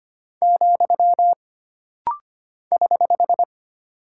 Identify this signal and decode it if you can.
Morse code, heard as a keyed tone, about 25 wpm, 700 Hz. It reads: , <HH>
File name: Single Punctuation-Prosign - Rapid-Fire - 25wpm